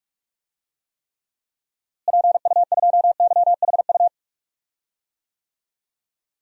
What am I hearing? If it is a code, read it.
Morse code, heard as a keyed tone, about 41 wpm, 700 Hz. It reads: PU2YHU
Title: DXCC-100 Call Signs - 6 Characters - Repeated 2x - 45wpm